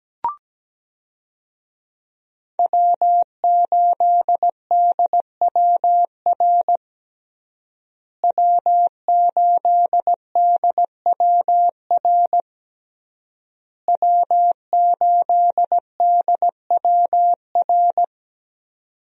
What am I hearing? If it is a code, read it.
Morse code, heard as a keyed tone, about 17 wpm, 700 Hz. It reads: W8DWR W8DWR W8DWR
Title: USA Call Signs - 5 Characters - Repeated 3x - 17wpm